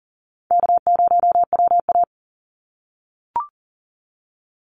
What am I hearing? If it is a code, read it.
Morse code, heard as a keyed tone, about 40 wpm, 700 Hz. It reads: K0WA
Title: Call Signs - 40wpm